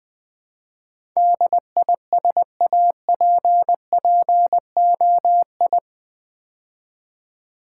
Morse code, heard as a keyed tone, about 20 wpm, 700 Hz. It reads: DISAPPOI
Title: Build A Word - 4 Letters At A Time - 10-Letter Words - 20wpm